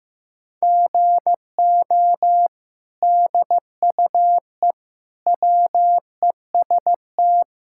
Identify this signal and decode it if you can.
Morse code, heard as a keyed tone, about 16 wpm, 700 Hz. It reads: GO DUE WEST
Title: Sentences from Top 1000 Words - Review Part 6 - 15wpm